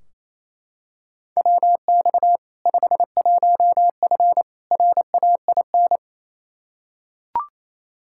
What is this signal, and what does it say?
Morse code, heard as a keyed tone, about 27 wpm, 700 Hz. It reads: WX 51F RAIN
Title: K1USN SST QSO Practice 28wpm